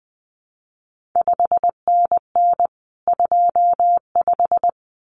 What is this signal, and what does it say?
Morse code, heard as a keyed tone, about 20 wpm, 700 Hz. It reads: 5NN 25
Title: CQ WW DX Contest Exchange - 20wpm